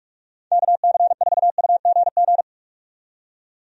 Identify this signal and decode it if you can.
Morse code, heard as a keyed tone, about 43 wpm, 700 Hz. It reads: KC4UGG
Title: USA Call Signs - 6 Characters - 45wpm